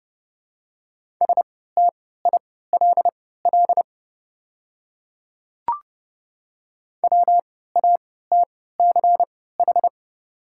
Morse code, heard as a keyed tone, about 30 wpm, spaced as for 17 wpm, 700 Hz. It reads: STILL WATCH
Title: 3x Character Spacing - Top 500 Words - 30wpm